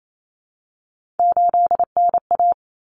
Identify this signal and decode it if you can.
Morse code, heard as a keyed tone, about 28 wpm, 700 Hz. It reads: 8NA
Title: Common Cut Numbers - 3-Digit 28wpm